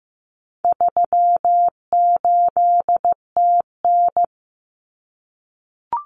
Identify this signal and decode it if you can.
Morse code, heard as a keyed tone, about 15 wpm, 700 Hz. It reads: 38TN
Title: Common Cut Numbers - 4-Digit 15wpm